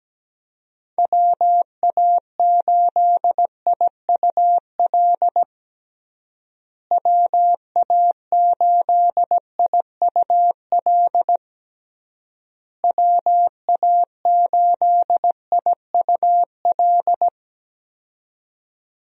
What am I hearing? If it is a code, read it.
Morse code, heard as a keyed tone, about 17 wpm, 700 Hz. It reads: WA8IUL WA8IUL WA8IUL
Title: USA Call Signs - 6 Characters - Repeated 3x - 17wpm